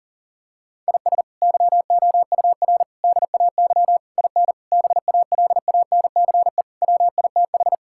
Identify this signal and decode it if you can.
Morse code, heard as a keyed tone, about 40 wpm, 690 Hz. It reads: IS YOUR DAY IN BALANCE WITH
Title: Sentences from Top 1000 Words - Review Part 6 - 40wpm